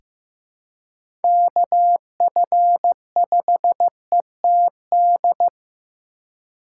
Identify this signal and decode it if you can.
Morse code, heard as a keyed tone, about 15 wpm, 700 Hz. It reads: KF5ETD
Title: K1USN SST QSO Practice - 15wpm